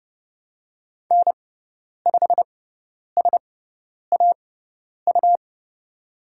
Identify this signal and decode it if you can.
Morse code, heard as a keyed tone, about 30 wpm, spaced as for 10 wpm, 690 Hz. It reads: N5SAU
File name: Lesson 18g - Letter m - Call Signs 30wpm